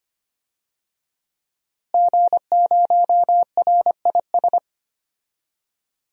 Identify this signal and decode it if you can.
Morse code, heard as a keyed tone, about 25 wpm, 700 Hz. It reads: G0RIS
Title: DXCC-100 Call Signs - 5 Characters - Repeated 3x - 25wpm